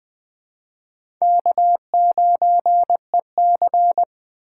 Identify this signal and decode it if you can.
Morse code, heard as a keyed tone, about 20 wpm, 700 Hz. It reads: K9EC
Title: USA Call Signs - 4 Characters - 2x Character Spacing - 20wpm